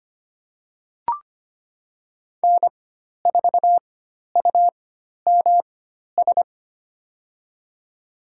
Morse code, heard as a keyed tone, about 25 wpm, spaced as for 12 wpm, 700 Hz. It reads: N4UMS
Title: USA Call Signs - 5 Characters - 4x Character Spacing - 25wpm